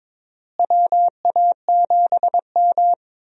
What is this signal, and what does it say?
Morse code, heard as a keyed tone, about 22 wpm, 700 Hz. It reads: WA7M
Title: USA Call Signs - 4 Characters - Repeated 2x - 22wpm